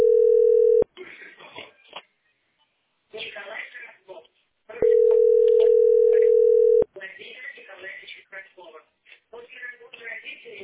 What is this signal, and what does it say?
Ringback tone (North America), 440 + 480 Hz, with speech.
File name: 6e -e72c-4de3-a55f